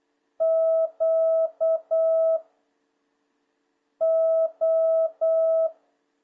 Morse code, heard as a keyed tone, about 8 wpm, 650 Hz. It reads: Q O
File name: jh8cbh / 2026cw